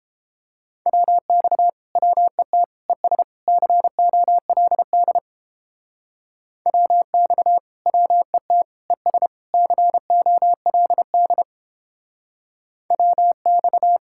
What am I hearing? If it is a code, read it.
Morse code, heard as a keyed tone, about 29 wpm, 700 Hz. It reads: WX WET ES COLD WX WET ES COLD WX